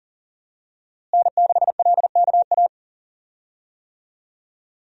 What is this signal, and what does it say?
Morse code, heard as a keyed tone, about 37 wpm, 700 Hz. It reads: N6LKA
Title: USA Call Signs - 5 Characters - Repeated 3x - 40wpm